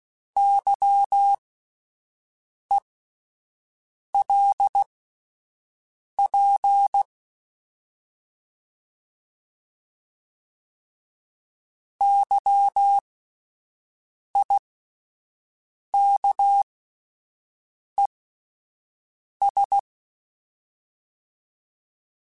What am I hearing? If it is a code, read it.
Morse code, heard as a keyed tone, about 16 wpm, spaced as for 5 wpm, 800 Hz. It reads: YELP YIKES